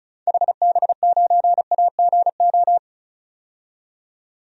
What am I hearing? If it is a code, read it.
Morse code, heard as a keyed tone, about 34 wpm, 700 Hz. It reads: HB9AGO
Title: DXCC-100 Call Signs - 6 Characters - 35wpm